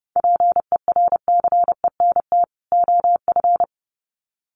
Morse code, heard as a keyed tone, about 29 wpm, 700 Hz. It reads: PERCENT OF
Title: Sets of 2 Words 30wpm